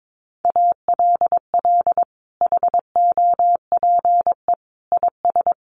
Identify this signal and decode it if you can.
Morse code, heard as a keyed tone, about 22 wpm, 700 Hz. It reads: ALL HOPE IS
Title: Sentences from Top 1400 Words 22wpm